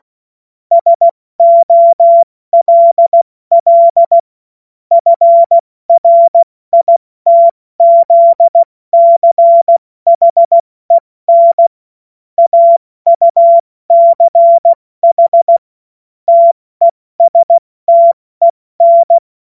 Morse code, heard as a keyed tone, about 16 wpm, 680 Hz. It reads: SOLL FRITZCHEN AUCH TESTEN
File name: Morsekurs / Witze / Witz 13. September 2022 70 F80